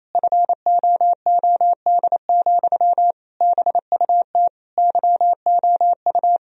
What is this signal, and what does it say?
Morse code, heard as a keyed tone, about 28 wpm, 700 Hz. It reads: FOOD, BUT YOU
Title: Sentences from Top 1000 Words - Review Part 1 - 28wpm